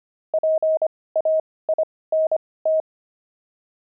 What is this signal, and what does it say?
Morse code, heard as a keyed tone, about 25 wpm, spaced as for 18 wpm, 630 Hz. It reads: PAINT
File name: pangram-fs-18-cs-25-oxide